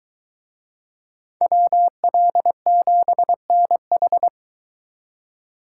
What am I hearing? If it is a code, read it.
Morse code, heard as a keyed tone, about 21 wpm, 700 Hz. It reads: WL7NH